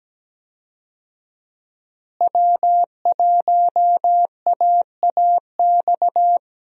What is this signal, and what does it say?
Morse code, heard as a keyed tone, about 17 wpm, 700 Hz. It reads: W1AAX
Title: USA Call Signs - 5 Characters - Repeated 2x - 17wpm